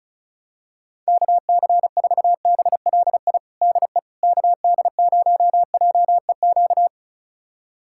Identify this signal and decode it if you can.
Morse code, heard as a keyed tone, about 36 wpm, 700 Hz. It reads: KC4BLI DE KD0JEQ
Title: USA Callsign DE Callsign 35wpm